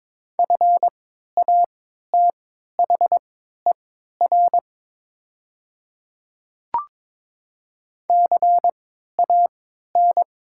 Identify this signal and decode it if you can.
Morse code, heard as a keyed tone, about 22 wpm, spaced as for 12 wpm, 700 Hz. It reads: FATHER CAN
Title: 3x Character Spacing - Top 300 Words - 22wpm